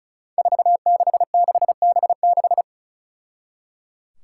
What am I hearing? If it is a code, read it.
Morse code, heard as a keyed tone, about 36 wpm, 700 Hz. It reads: 466B6
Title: Five Challenging Letter-Number Combinations 35wpm